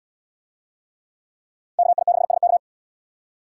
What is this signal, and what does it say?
Morse code, heard as a keyed tone, about 73 wpm, 700 Hz. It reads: KI6SK